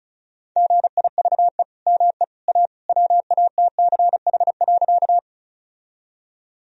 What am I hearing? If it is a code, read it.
Morse code, heard as a keyed tone, about 36 wpm, 700 Hz. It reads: GIVE ME A WATCH.
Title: Sentences from Top 1000 Words - Review Part 1 - 35wpm